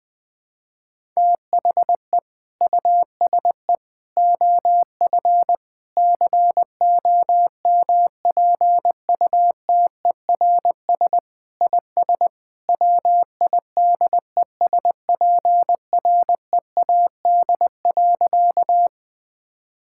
Morse code, heard as a keyed tone, about 20 wpm, 700 Hz. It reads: THE USE OF COMPUTERS IS WIDESPREAD.
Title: Sentences with 3-Word Phrases - 20wpm